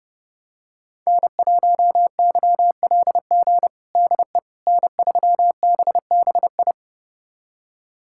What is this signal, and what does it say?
Morse code, heard as a keyed tone, about 29 wpm, 700 Hz. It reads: N1YLG DE N3BBI
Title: USA Callsign DE Callsign - 5 Characters - 30wpm